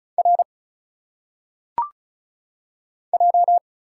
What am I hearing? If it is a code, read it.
Morse code, heard as a keyed tone, about 34 wpm, 700 Hz. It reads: R J
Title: Single Letters - Phonetic Pronunciation 35wpm